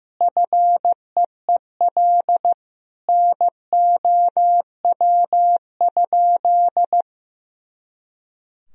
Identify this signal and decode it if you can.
Morse code, heard as a keyed tone, about 16 wpm, 700 Hz. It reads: FEEL NOW?
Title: Sentences from Top 400 Words and Abbreviations 15wpm